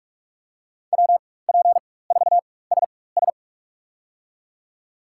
Morse code, heard as a keyed tone, about 43 wpm, spaced as for 21 wpm, 700 Hz. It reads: WP4SS
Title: USA Call Signs - 5 Characters - 4x Character Spacing - 45wpm